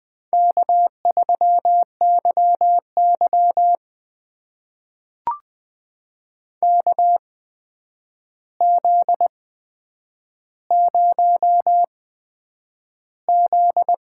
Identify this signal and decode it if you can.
Morse code, heard as a keyed tone, about 20 wpm, spaced as for 9 wpm, 700 Hz. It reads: K3YY KZ0Z
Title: USA Call Signs - 4 Characters - 8x Character Spacing - 20wpm